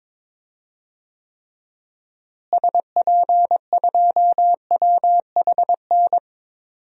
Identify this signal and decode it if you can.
Morse code, heard as a keyed tone, about 22 wpm, 700 Hz. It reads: SP2WHN
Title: DXCC-100 Call Signs - 6 Characters - Repeated 2x - 22wpm